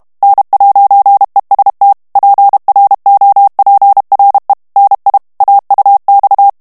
Morse code, heard as a keyed tone, about 32 wpm, 800 Hz. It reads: N'EST PROPRE NI AUX
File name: CW / FICHIERS-MP3 / FRANCAIS TEXTE CLAIR 2 VITESSE 30 MOTS MINUTES